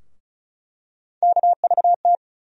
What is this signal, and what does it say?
Morse code, heard as a keyed tone, about 36 wpm, 700 Hz. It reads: KVT
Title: USA Call Sign Postfixes - 3 Letters 35wpm